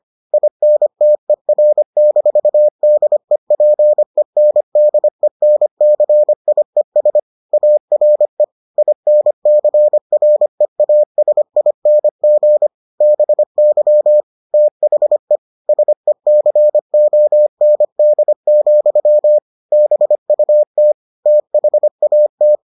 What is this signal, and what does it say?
Morse code, heard as a keyed tone, about 25 wpm, 600 Hz. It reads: INTER-DEPENDENCIES ARE INCREASING BY THE SECOND, BUT THAT